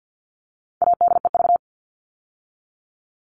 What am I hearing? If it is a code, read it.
Morse code, heard as a keyed tone, about 50 wpm, 700 Hz. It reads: ADEV
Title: All Cut Numbers - 4-Digit 50wpm